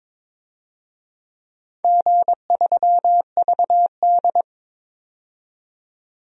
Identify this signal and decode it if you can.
Morse code, heard as a keyed tone, about 22 wpm, 700 Hz. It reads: G3VD
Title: DXCC-100 Call Signs - 4 Characters - 22wpm